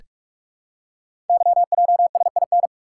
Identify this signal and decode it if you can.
Morse code, heard as a keyed tone, about 44 wpm, 690 Hz. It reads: YJSIN
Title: Code Groups - 5 Letters - 45wpm